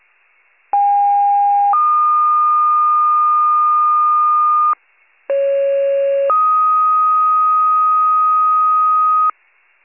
Two-tone page, heard as a one-second tone, then a three-second tone, about 810 then 1220 Hz, about 550 then 1250 Hz. It is heard from 0.7 to 9.3 s.
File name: pager / 28 FIRE / wcalert